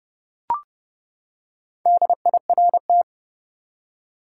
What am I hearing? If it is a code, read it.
Morse code, heard as a keyed tone, about 30 wpm, 700 Hz. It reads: DIRT
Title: Four Letter Words - 30wpm